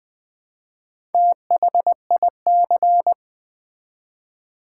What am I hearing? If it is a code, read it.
Morse code, heard as a keyed tone, about 20 wpm, 700 Hz. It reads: THIC
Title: Build A Word - 1 Letter At A Time - 3 to 5-Letter Words - 20wpm